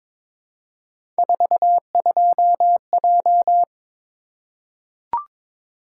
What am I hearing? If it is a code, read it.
Morse code, heard as a keyed tone, about 22 wpm, 700 Hz. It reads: 42J
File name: practice / 22wpm / Three Letter-Number Combinations - 22wpm